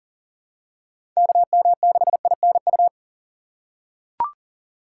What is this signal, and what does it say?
Morse code, heard as a keyed tone, about 40 wpm, 700 Hz. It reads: KM6INU
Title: USA Call Signs - 6 Characters - 2x Character Spacing - 40wpm